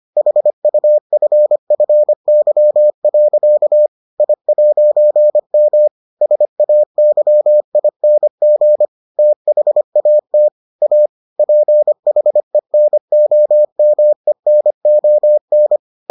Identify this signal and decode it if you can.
Morse code, heard as a keyed tone, about 25 wpm, 600 Hz. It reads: HUFFY. I'M SAYING THAT A PHENOMENON